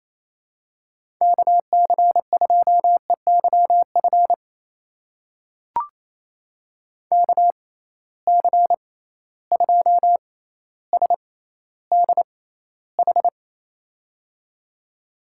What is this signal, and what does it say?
Morse code, heard as a keyed tone, about 28 wpm, spaced as for 14 wpm, 700 Hz. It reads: KC2EYF KC2SDH
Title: USA Call Signs - 6 Characters - 6x Character Spacing - 28wpm